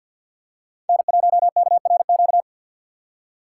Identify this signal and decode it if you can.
Morse code, heard as a keyed tone, about 48 wpm, 700 Hz. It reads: N1KRQ
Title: USA Call Signs - 5 Characters - 6x Character Spacing - 50wpm